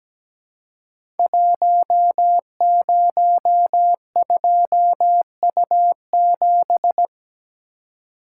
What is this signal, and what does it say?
Morse code, heard as a keyed tone, about 17 wpm, 700 Hz. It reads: 102U7
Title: Code Groups - 5 Letter:Numbers - 17wpm